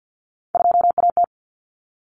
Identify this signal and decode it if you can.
Morse code, heard as a keyed tone, about 50 wpm, 700 Hz. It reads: 2AT
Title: Common Cut Numbers - 3-Digit 50wpm